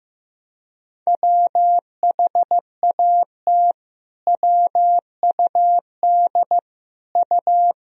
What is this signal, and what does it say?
Morse code, heard as a keyed tone, about 15 wpm, 700 Hz. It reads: WHAT WUD U
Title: Sentences from Top 300 Words and Abbreviations 15wpm